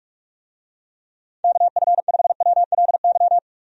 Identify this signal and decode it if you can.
Morse code, heard as a keyed tone, about 44 wpm, 700 Hz. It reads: KF5WLY
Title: USA Call Signs - 6 Characters - 45wpm